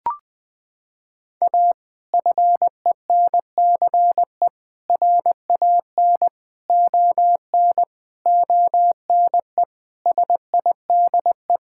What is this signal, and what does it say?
Morse code, heard as a keyed tone, about 20 wpm, 700 Hz. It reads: A FENCE RAN ON ONE SIDE
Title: Sentences with 5-Word Phrases - 20wpm